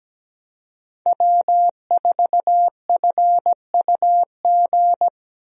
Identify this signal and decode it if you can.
Morse code, heard as a keyed tone, about 17 wpm, 700 Hz. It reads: W4FUG
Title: USA Call Signs - 5 Characters - 17wpm